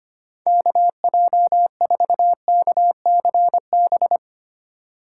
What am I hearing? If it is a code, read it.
Morse code, heard as a keyed tone, about 25 wpm, 700 Hz. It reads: KJ4KCB
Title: USA Call Signs - 6 Characters - 3x Character Spacing - 25wpm